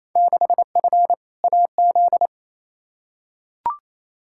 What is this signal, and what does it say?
Morse code, heard as a keyed tone, about 28 wpm, 700 Hz. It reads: 6F AZ
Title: Field Day Exchanges Encore - 28wpm